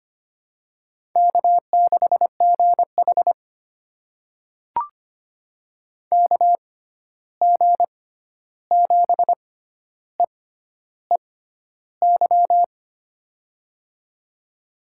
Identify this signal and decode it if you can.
Morse code, heard as a keyed tone, about 25 wpm, spaced as for 11 wpm, 700 Hz. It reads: K6GH KG7EEY
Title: USA Call Signs - 6x Character Spacing - 25wpm